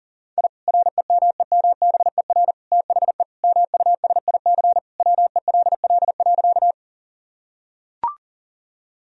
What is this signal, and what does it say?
Morse code, heard as a keyed tone, about 37 wpm, 700 Hz. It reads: I REMEMBER THE MUSIC WELL.
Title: Sentences from Top 1000 Words - Review Part 1 - 40wpm